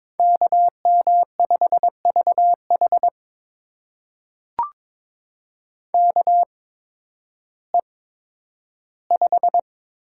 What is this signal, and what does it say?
Morse code, heard as a keyed tone, about 22 wpm, spaced as for 12 wpm, 700 Hz. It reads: KM5VH KE5
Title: USA Call Signs - 8x Character Spacing - 22wpm